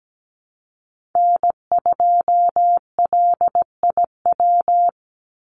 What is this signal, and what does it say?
Morse code, heard as a keyed tone, about 17 wpm, 700 Hz. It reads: N2LIW
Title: USA Call Signs 17wpm